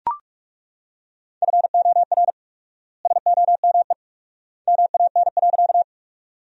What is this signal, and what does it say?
Morse code, heard as a keyed tone, about 43 wpm, 700 Hz. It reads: FOR SOME MAN.
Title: 4x Word-Spacing - Sentences from Top 200 Words - 45wpm